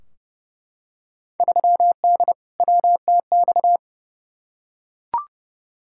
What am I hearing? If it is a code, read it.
Morse code, heard as a keyed tone, about 30 wpm, 700 Hz. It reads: 3D WTX
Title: Field Day Exchanges Encore - 30wpm